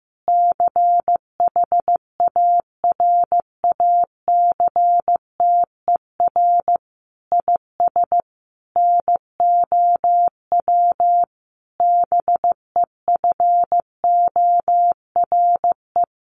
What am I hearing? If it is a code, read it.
Morse code, heard as a keyed tone, about 14 wpm, 700 Hz. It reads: CHARACTER IS NOW BEFORE
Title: Sentences from Top 1200 Words 15wpm